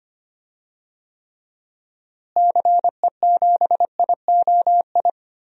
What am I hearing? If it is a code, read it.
Morse code, heard as a keyed tone, about 25 wpm, 700 Hz. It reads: CE7IOI
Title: DXCC-100 Call Signs - 6 Characters - Repeated 3x - 25wpm